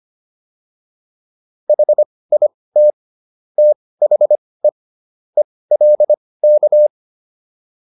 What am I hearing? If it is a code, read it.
Morse code, heard as a keyed tone, about 25 wpm, spaced as for 18 wpm, 600 Hz. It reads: HIT THE ELK